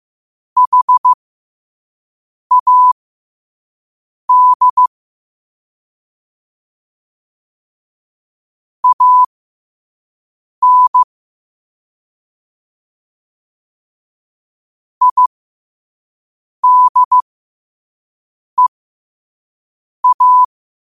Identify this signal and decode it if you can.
Morse code, heard as a keyed tone, about 15 wpm, spaced as for 5 wpm, 1000 Hz. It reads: HAD AN IDEA
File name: qotdfiles / 5wpm